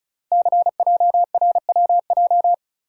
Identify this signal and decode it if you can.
Morse code, heard as a keyed tone, about 36 wpm, 690 Hz. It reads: CJRWJ
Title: Code Groups - 5 Letters - 35wpm